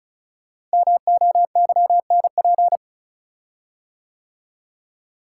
Morse code, heard as a keyed tone, about 36 wpm, 700 Hz. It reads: MOYNP
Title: Code Groups - 5 Letter:Numbers - 35wpm